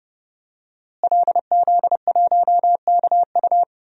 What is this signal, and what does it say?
Morse code, heard as a keyed tone, about 32 wpm, 700 Hz. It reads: LZ1KU
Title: DXCC-100 Call Signs - 5 Characters - Repeated 3x - 30wpm